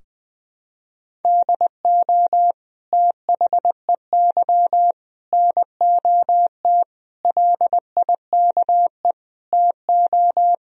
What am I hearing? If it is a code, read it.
Morse code, heard as a keyed tone, about 21 wpm, 700 Hz. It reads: DO THEY NOT LIKE TO
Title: Sentences from Top 1000 Words - Review Part 4 - 20wpm